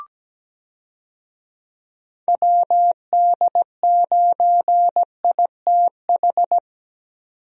Morse code, heard as a keyed tone, about 17 wpm, 700 Hz. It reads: WD9ITH
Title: USA Call Signs - 6 Characters - 17wpm